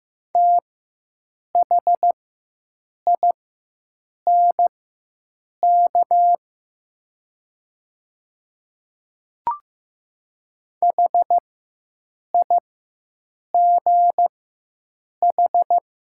Morse code, heard as a keyed tone, about 15 wpm, spaced as for 7 wpm, 700 Hz. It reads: THINK HIGH
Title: 4x Character Spacing - Top 200 Words - 15wpm